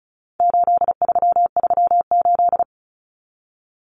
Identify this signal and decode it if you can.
Morse code, heard as a keyed tone, about 34 wpm, 700 Hz. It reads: 8338
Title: Numbers - 4-Digit 35wpm